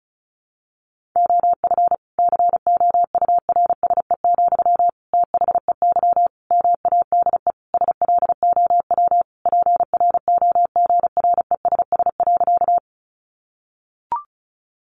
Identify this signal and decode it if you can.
Morse code, heard as a keyed tone, about 34 wpm, 700 Hz. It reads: OF COURSE, THEY MADE SLOW PROGRESS.
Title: Sentences from Top 1400 Words 35wpm